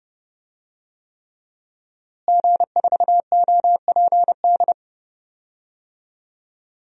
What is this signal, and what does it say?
Morse code, heard as a keyed tone, about 31 wpm, 700 Hz. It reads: G4OPD
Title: DXCC-100 Call Signs - 5 Characters - Repeated 3x - 30wpm